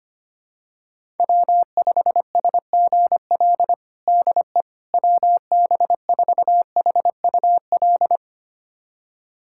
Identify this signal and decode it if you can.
Morse code, heard as a keyed tone, about 25 wpm, 700 Hz. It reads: W5SGL DE WB4HUL
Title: USA Callsign DE Callsign 25wpm